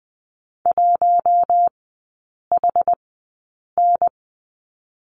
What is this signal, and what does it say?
Morse code, heard as a keyed tone, about 20 wpm, 700 Hz. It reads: 1 H N
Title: 172 Lesson 28e - Number 9 - Three Characters 20wpm